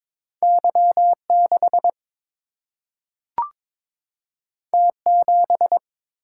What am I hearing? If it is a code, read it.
Morse code, heard as a keyed tone, about 22 wpm, 700 Hz. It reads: Y6 T7
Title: Single-Letter Single-Number 22wpm